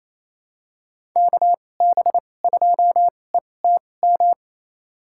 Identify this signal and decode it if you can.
Morse code, heard as a keyed tone, about 28 wpm, spaced as for 20 wpm, 700 Hz. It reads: KB2ETM
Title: USA Call Signs - 6 Characters - 2x Character Spacing - 28wpm